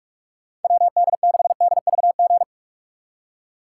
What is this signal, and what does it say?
Morse code, heard as a keyed tone, about 41 wpm, 700 Hz. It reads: WD6DVG
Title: USA Call Signs - 6 Characters - 3x Character Spacing - 45wpm